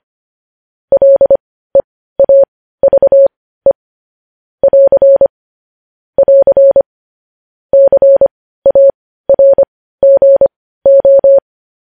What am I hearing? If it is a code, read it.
Morse code, heard as a keyed tone, about 25 wpm, spaced as for 15 wpm, 550 Hz. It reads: LEAVE <AR> <AR> CARGO
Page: News Headlines in Morse Code at 15 WPM